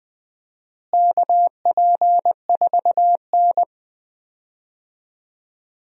Morse code, heard as a keyed tone, about 21 wpm, 700 Hz. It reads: KP4N
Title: DXCC-100 Call Signs - 4 Characters - Repeated 3x - 20wpm